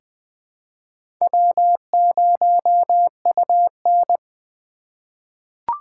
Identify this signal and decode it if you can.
Morse code, heard as a keyed tone, about 21 wpm, 700 Hz. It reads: W0UN
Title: USA Call Signs - 4 Characters - 2x Character Spacing - 20wpm